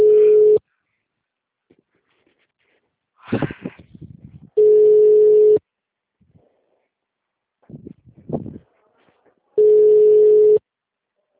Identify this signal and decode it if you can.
Ringback tone (Europe, 425 Hz).